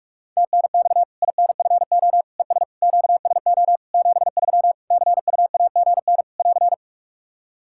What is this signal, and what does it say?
Morse code, heard as a keyed tone, about 44 wpm, 700 Hz. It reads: TNX INFO ES QSO 73 CUAGN <AR>
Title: Practice QSO - Exchange 4 - Ending - 45wpm